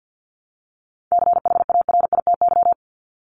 Morse code, heard as a keyed tone, about 50 wpm, 700 Hz. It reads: CHARITY
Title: Top 1501-1750 Words 50wpm